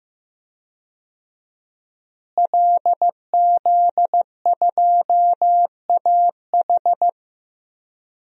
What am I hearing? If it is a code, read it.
Morse code, heard as a keyed tone, about 15 wpm, 700 Hz. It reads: LZ2AH
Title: DXCC-100 Call Signs - 5 Characters - Repeated 2x - 15wpm